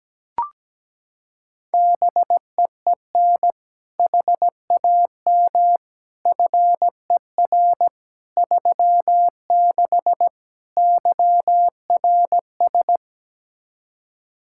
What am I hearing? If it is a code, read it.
Morse code, heard as a keyed tone, about 17 wpm, 700 Hz. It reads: BEEN HAM FER 36 YRS
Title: Practice QSO - Exchange 3 - More Info - 17wpm